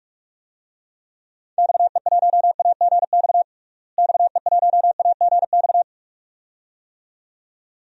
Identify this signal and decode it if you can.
Morse code, heard as a keyed tone, about 42 wpm, 700 Hz. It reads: XE1AGX XE1AGX
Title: DXCC-100 Call Signs - 6 Characters - Repeated 2x - 45wpm